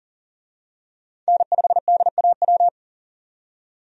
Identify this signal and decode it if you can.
Morse code, heard as a keyed tone, about 40 wpm, 700 Hz. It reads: N5DAW
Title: USA Call Signs - 5 Characters - 40wpm